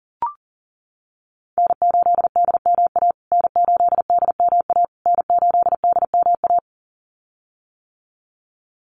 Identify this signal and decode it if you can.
Morse code, heard as a keyed tone, about 40 wpm, 700 Hz. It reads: N8DMA N8DMA N8DMA
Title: USA Call Signs Repeated 3x 40wpm